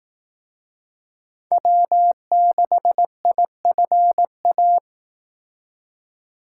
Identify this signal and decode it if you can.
Morse code, heard as a keyed tone, about 18 wpm, 700 Hz. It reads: W6IFA
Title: Speed Racing - USA 1x3 Call Signs 15wpm